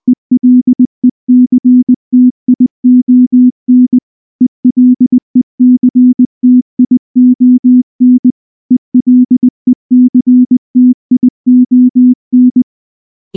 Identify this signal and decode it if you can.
Morse code, heard as a keyed tone, about 19 wpm, 250 Hz. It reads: ELECTION ELECTION ELECTION
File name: morse / morse-training-20WPM-3